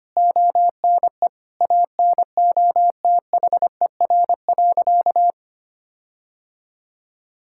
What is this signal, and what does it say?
Morse code, heard as a keyed tone, about 25 wpm, 700 Hz. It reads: ONE ANOTHER.
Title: Sentences from Top 1600 Words 25wpm